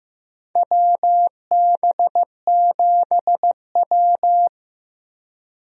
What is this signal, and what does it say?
Morse code, heard as a keyed tone, about 15 wpm, 700 Hz. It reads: WB7W
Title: USA Call Signs - 4 Characters - Repeated 3x - 15wpm